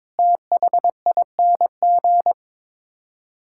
Morse code, heard as a keyed tone, about 22 wpm, 700 Hz. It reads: THING
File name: practice / 22wpm / Sets of 7 Words - 22wpm